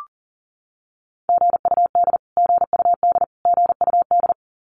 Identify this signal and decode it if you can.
Morse code, heard as a keyed tone, about 40 wpm, 700 Hz. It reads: GUD GUD GUD
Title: Abbreviations Repeated 3x - 40wpm